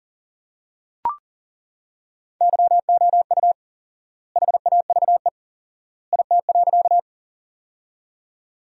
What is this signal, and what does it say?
Morse code, heard as a keyed tone, about 40 wpm, 700 Hz. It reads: YOU HAVE IT.
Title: 4x Word-Spacing - Sentences from Top 100 Words - 40wpm